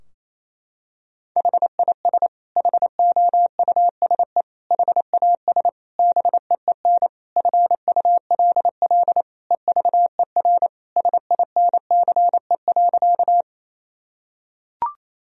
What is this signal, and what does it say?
Morse code, heard as a keyed tone, about 28 wpm, 700 Hz. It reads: HIS HOUSE HAS BEEN FULL EVER SINCE.
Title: Sentences from Top 1000 Words - Review Part 3 - 28wpm